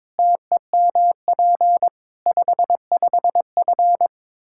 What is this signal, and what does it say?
Morse code, heard as a keyed tone, about 22 wpm, 700 Hz. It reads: TEMP 55F
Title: K1USN SST QSO Practice 22wpm